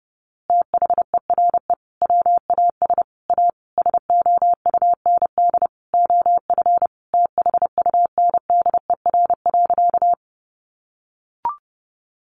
Morse code, heard as a keyed tone, about 30 wpm, 700 Hz. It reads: THERE WAS A SOUND OF THUNDER.
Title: A Sound Of Thunder 30wpm